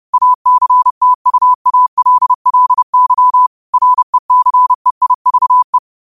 Morse code, heard as a keyed tone, about 30 wpm, 1000 Hz. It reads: ACTUALLY RECEIVE
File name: qotdfiles / 30wpm